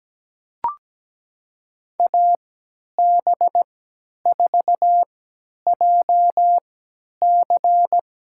Morse code, heard as a keyed tone, about 17 wpm, spaced as for 10 wpm, 700 Hz. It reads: AB4JC
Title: USA Call Signs - 5 Characters - 3x Character Spacing - 17wpm